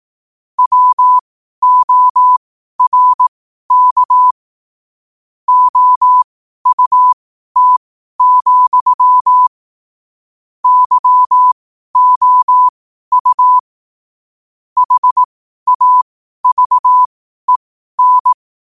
Morse code, heard as a keyed tone, about 18 wpm, spaced as for 12 wpm, 1000 Hz. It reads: WORK OUT, YOU HAVEN
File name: qotdfiles / 12wpm